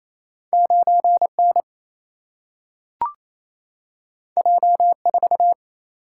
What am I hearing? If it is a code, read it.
Morse code, heard as a keyed tone, about 28 wpm, 700 Hz. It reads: 9 N J 4
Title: Single-Letter Single-Number 28wpm